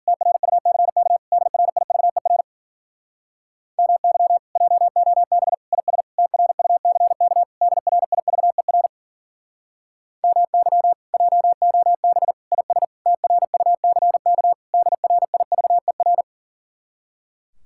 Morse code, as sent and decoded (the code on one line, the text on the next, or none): - .-. ..- -.-. -.- / -.. .-. .. ...- . .-. / -- -.-- / .--- --- -... / .. ... / - .-. ..- -.-. -.- / -.. .-. .. ...- . .-. / -- -.-- / .--- --- -... / .. ... / - .-. ..- -.-. -.- / -.. .-. .. ...- . .-.
TRUCK DRIVER MY JOB IS TRUCK DRIVER MY JOB IS TRUCK DRIVER